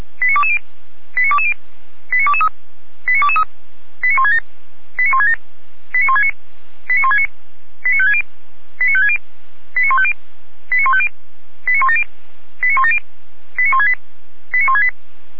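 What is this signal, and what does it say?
Five-tone selective call (ZVEI), heard as quick runs of five tones, tones of about 70 ms. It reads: 89220 89220 89223 89223 89167 89167 89169 89169 89560 89560 89150 89150 89170 89170 89168 89168
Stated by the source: ZVEI five-tone call 89011 received on an SDR.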